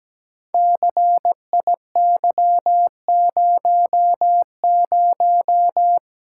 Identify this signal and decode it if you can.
Morse code, heard as a keyed tone, about 17 wpm, 700 Hz. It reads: CIY00
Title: Code Groups - 5 Letter:Numbers - 17wpm